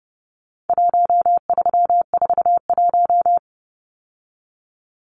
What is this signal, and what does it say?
Morse code, heard as a keyed tone, about 30 wpm, 700 Hz. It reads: 1341
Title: Numbers - 4-Digit 30wpm